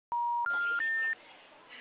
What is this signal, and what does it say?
Special information tone.